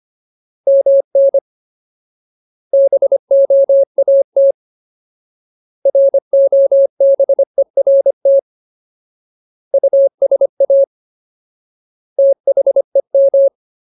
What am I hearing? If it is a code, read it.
Morse code, heard as a keyed tone, about 25 wpm, 550 Hz. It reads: MN BOAT ROBERT USA THEM
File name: Ultimate Word List / 25WPM Ultimate